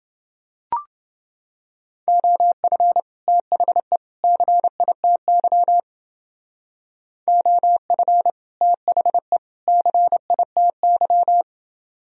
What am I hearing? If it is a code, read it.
Morse code, heard as a keyed tone, about 30 wpm, 700 Hz. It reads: OF THE CITY OF THE CITY